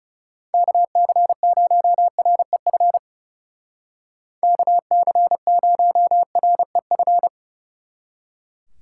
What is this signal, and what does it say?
Morse code, heard as a keyed tone, about 37 wpm, 700 Hz. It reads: KC0REF KC0REF
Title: Speed Racing - USA 2x3 Call Signs 30wpm